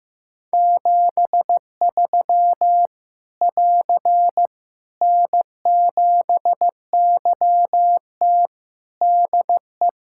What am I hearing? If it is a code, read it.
Morse code, heard as a keyed tone, about 15 wpm, 700 Hz. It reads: 73 <AR> N7YT DE
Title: Practice QSO - Exchange 4 - Ending - 15wpm